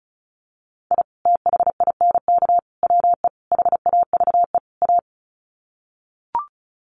Morse code, heard as a keyed tone, about 34 wpm, 700 Hz. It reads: I THINK WE HAVE A
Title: Sets of 5 Words Encore 35wpm